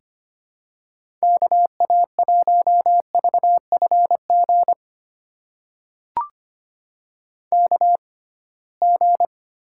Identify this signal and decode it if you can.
Morse code, heard as a keyed tone, about 25 wpm, spaced as for 19 wpm, 700 Hz. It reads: KA1VFG KG
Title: USA Call Signs - 6 Characters - 6x Character Spacing - 25wpm